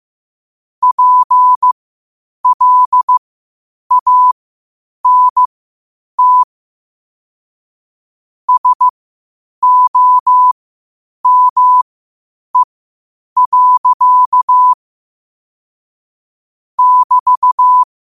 Morse code, heard as a keyed tone, about 15 wpm, spaced as for 8 wpm, 1000 Hz. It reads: PLANT SOME. <BT>
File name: qotdfiles / 8wpm